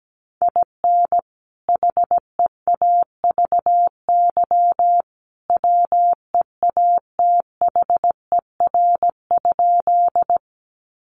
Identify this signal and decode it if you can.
Morse code, heard as a keyed tone, about 17 wpm, 700 Hz. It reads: IN HEAVY WEATHER?
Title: Sentences from Top 1100 Words 17wpm